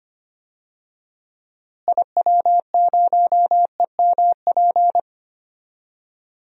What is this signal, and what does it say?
Morse code, heard as a keyed tone, about 25 wpm, 700 Hz. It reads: IW0EMP
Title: DXCC-100 Call Signs - 6 Characters - Repeated 3x - 25wpm